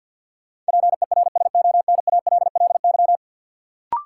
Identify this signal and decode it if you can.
Morse code, heard as a keyed tone, about 46 wpm, 700 Hz. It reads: PERSONALLY
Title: Sentences from Top 1400 Words - New Word Drill - 50wpm